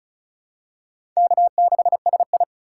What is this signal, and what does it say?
Morse code, heard as a keyed tone, about 36 wpm, 700 Hz. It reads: K6SI
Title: USA Call Signs - 2x Character Spacing - 35wpm